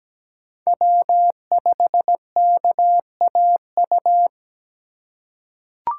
Morse code, heard as a keyed tone, about 17 wpm, 700 Hz. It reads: W5KAU
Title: USA Call Signs - 5 Characters - 4x Character Spacing - 17wpm